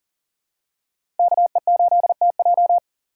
Morse code, heard as a keyed tone, about 40 wpm, 700 Hz. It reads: KE8TJ
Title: USA Call Signs - 5 Characters - 4x Character Spacing - 40wpm